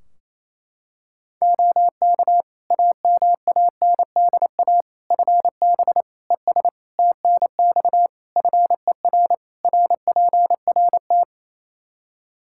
Morse code, heard as a keyed tone, about 28 wpm, 700 Hz. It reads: OK AMANDA FB ES TNX FER RPRT
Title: K1USN SST QSO Practice 28wpm